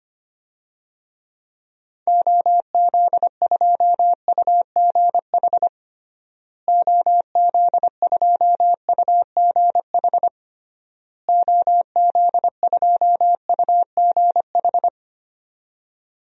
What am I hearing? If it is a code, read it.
Morse code, heard as a keyed tone, about 25 wpm, 700 Hz. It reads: OZ2UGH OZ2UGH OZ2UGH
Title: DXCC-100 Call Signs - 6 Characters - Repeated 3x - 25wpm